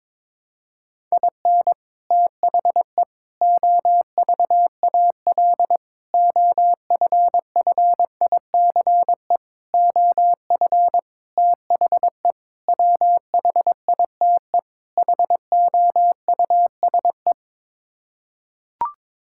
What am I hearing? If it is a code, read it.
Morse code, heard as a keyed tone, about 22 wpm, 700 Hz. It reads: IN THE OVAL OFFICE OF THE WHITE HOUSE
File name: practice / 22wpm / Sets of 8 Words - 22wpm